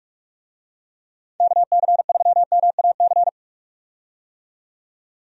Morse code, heard as a keyed tone, about 44 wpm, 700 Hz. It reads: KC3MAC
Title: USA Call Signs - 6 Characters - 45wpm